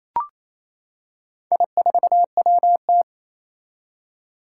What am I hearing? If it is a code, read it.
Morse code, heard as a keyed tone, about 27 wpm, 700 Hz. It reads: I4WT
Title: Four Challenging Letter-Number Combinations 28wpm